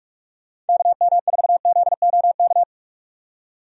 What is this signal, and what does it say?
Morse code, heard as a keyed tone, about 46 wpm, 700 Hz. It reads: KM4ZOK
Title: USA Call Signs - 6 Characters - 45wpm